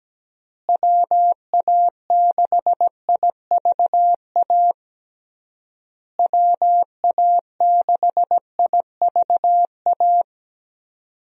Morse code, heard as a keyed tone, about 17 wpm, 700 Hz. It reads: WA6IVA WA6IVA
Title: USA Call Signs - 6 Characters - Repeated 3x - 17wpm